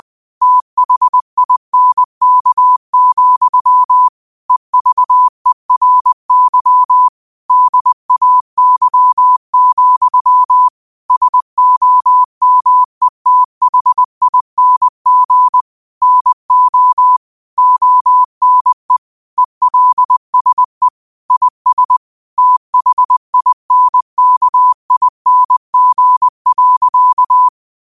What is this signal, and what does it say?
Morse code, heard as a keyed tone, about 20 wpm, 1000 Hz. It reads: THINK, EVERY DAY, SOMETHING NO ONE ELSE IS THINKING.
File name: qotdfiles / 20wpm